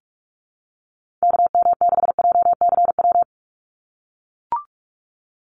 Morse code, heard as a keyed tone, about 44 wpm, 700 Hz. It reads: KM6JCW
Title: USA Call Signs 45wpm